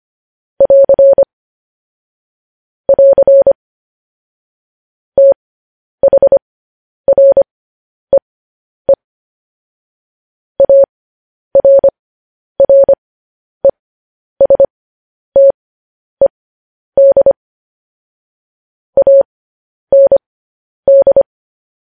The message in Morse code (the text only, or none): <AR> <AR> THREE ARRESTED AND